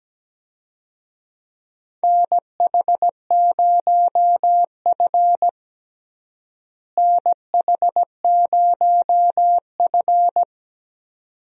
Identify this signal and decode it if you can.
Morse code, heard as a keyed tone, about 17 wpm, 700 Hz. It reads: NH0F NH0F
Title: USA Call Signs - 4 Characters - Repeated 2x - 17wpm